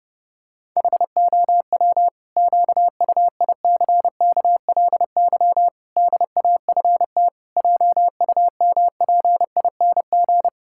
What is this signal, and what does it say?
Morse code, heard as a keyed tone, about 30 wpm, 700 Hz. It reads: HOW QUICKLY DAFT JUMPING
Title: Pangram Challenge v2 - 30wpm